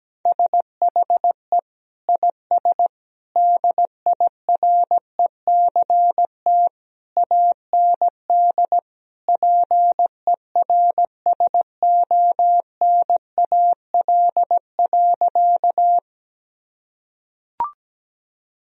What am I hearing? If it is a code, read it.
Morse code, heard as a keyed tone, about 17 wpm, 700 Hz. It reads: SHE IS DIRECT AND PERSONAL.
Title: Sentences from Top 1000 Words - Review Part 4 - 17wpm